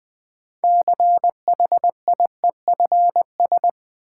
Morse code, heard as a keyed tone, about 21 wpm, 700 Hz. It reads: CHIEFS
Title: Sets of 5 Words Encore - New Word Drill - 20wpm